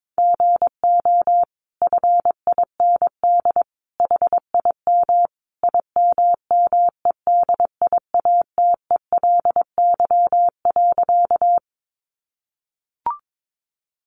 Morse code, heard as a keyed tone, about 22 wpm, 700 Hz. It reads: GO FIND HIM IMMEDIATELY.
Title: Sentences from Top 1100 Words 22wpm